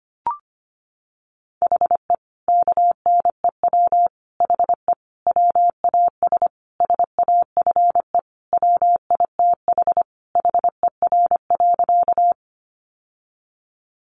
Morse code, heard as a keyed tone, about 25 wpm, 700 Hz. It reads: HE KNEW HE WAS SAFE WITH HER.
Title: Sentences from Top 900 Words 25wpm